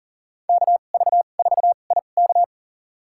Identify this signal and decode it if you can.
Morse code, heard as a keyed tone, about 40 wpm, spaced as for 29 wpm, 700 Hz. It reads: KV4IK
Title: USA Call Signs - 5 Characters - 2x Character Spacing - 40wpm